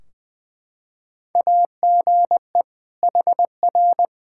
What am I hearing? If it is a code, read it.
Morse code, heard as a keyed tone, about 21 wpm, 700 Hz. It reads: AGE HR
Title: Practice QSO - Exchange 3 - More Info - 20wpm